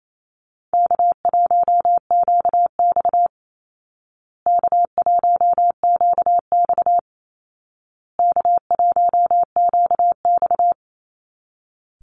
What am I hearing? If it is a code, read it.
Morse code, heard as a keyed tone, about 28 wpm, 700 Hz. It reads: K1QX K1QX K1QX
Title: Easy Call Signs Repeated 3x - 28wpm